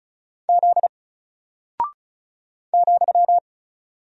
Morse code, heard as a keyed tone, about 36 wpm, 700 Hz. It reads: Z ,
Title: Single Character (Letter:Number:Punctuation:Prosign) Alt 35wpm